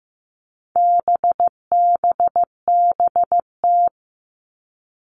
Morse code, heard as a keyed tone, about 15 wpm, 700 Hz. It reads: BBBT
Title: All Cut Numbers - 4-Digit 15wpm